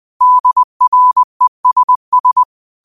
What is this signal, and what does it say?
Morse code, heard as a keyed tone, about 20 wpm, 1000 Hz. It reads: DRESS